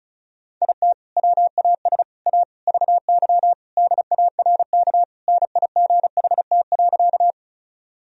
Morse code, heard as a keyed tone, about 36 wpm, 700 Hz. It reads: IT WAS A VY DARK NIGHT.
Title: Sentences from Top 500 Words and Abbreviations 35wpm